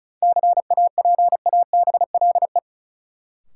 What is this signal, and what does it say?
Morse code, heard as a keyed tone, about 34 wpm, 700 Hz. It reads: CAPABLE
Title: Sentences from Top 1200 Words - New Word Drill - 35wpm